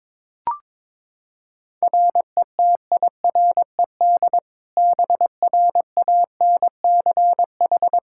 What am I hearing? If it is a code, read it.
Morse code, heard as a keyed tone, about 21 wpm, 700 Hz. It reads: RETIRED BRANCH
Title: Practice QSO - Exchange 3 - More Info - 22wpm